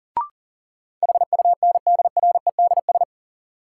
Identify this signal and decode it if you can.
Morse code, heard as a keyed tone, about 40 wpm, 700 Hz. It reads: HUNDREDS
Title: Sets of 4 Words - New Word Drill - 40wpm